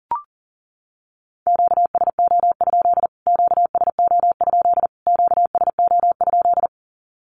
Morse code, heard as a keyed tone, about 40 wpm, 700 Hz. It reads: QSO? QSO? QSO?
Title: Q-Codes Repeated 3x 40wpm